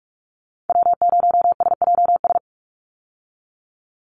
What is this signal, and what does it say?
Morse code, heard as a keyed tone, about 44 wpm, 700 Hz. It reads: W0SJS
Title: Call Signs Repeated 3x - 45wpm